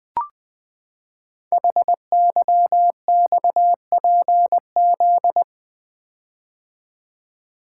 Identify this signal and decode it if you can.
Morse code, heard as a keyed tone, about 20 wpm, 700 Hz. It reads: HYXPZ
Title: Code Groups - 5 Letters - 20wpm